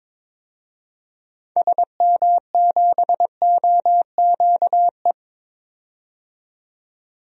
Morse code, heard as a keyed tone, about 22 wpm, 700 Hz. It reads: SM7OQE
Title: DXCC-100 Call Signs - 6 Characters - 22wpm